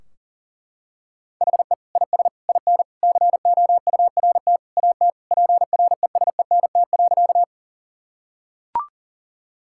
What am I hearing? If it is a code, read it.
Morse code, heard as a keyed tone, about 40 wpm, 700 Hz. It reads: HE IS IN COURT AT PRESENT.
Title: Sentences from Top 1600 Words 40wpm